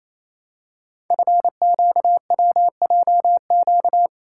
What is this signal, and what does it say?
Morse code, heard as a keyed tone, about 29 wpm, 700 Hz. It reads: FQWJQ
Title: Code Groups - 5 Letters - 28wpm